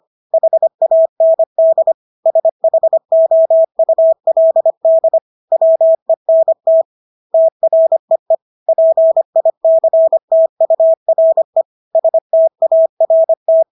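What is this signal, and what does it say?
Morse code, heard as a keyed tone, about 25 wpm, 650 Hz. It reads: HAND SHOULD WENT TREE PICTURE START
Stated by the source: Morse practice of English words at 25 wpm from CW med Gnister.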